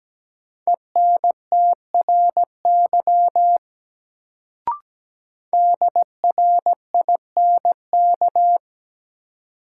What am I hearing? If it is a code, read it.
Morse code, heard as a keyed tone, about 17 wpm, 700 Hz. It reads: ENTRY DRINK
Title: Sentences from Top 1500 Words - New Word Drill - 17wpm